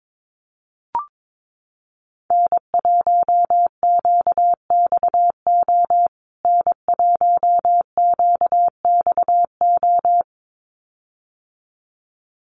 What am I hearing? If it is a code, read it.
Morse code, heard as a keyed tone, about 22 wpm, 700 Hz. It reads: N1QXO N1QXO
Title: USA Call Signs Repeated 2x 22wpm